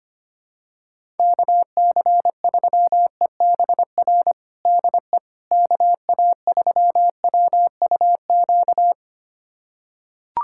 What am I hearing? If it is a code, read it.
Morse code, heard as a keyed tone, about 25 wpm, 700 Hz. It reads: KC3EBR DE KA3WUQ
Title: USA Callsign DE Callsign - 6 Characters - 25wpm